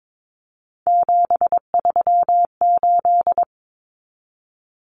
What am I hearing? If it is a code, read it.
Morse code, heard as a keyed tone, about 22 wpm, 700 Hz. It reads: 738
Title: Numbers - 3-Digit 22wpm